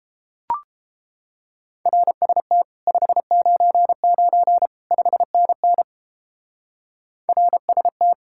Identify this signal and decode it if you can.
Morse code, heard as a keyed tone, about 29 wpm, 700 Hz. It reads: RST 599 5NN RST